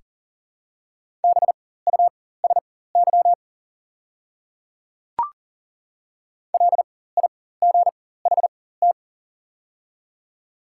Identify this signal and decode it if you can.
Morse code, heard as a keyed tone, about 40 wpm, spaced as for 19 wpm, 690 Hz. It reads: BUSY LIGHT
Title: 4x Character Spacing - Top 500 Words - 40wpm